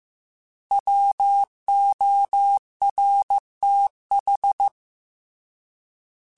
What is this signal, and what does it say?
Morse code, heard as a keyed tone, about 15 wpm, 800 Hz. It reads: WORTH
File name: Alphabetical / Alphabetical mp3s / Ws @13